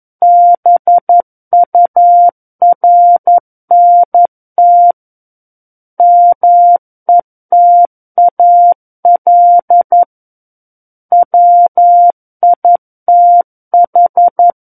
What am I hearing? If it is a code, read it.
Morse code, heard as a keyed tone, about 11 wpm, 700 Hz. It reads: BURNT METAL WITH
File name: 11 WPM